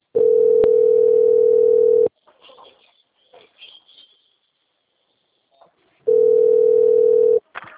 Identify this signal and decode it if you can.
Ringback tone (North America), 440 + 480 Hz.
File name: out- - - - - .2179